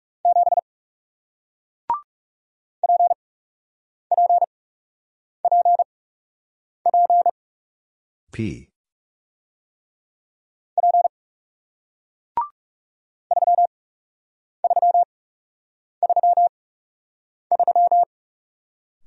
Morse code, heard as a keyed tone, about 40 wpm, spaced as for 8 wpm, 700 Hz. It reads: 7 PPPP P 3333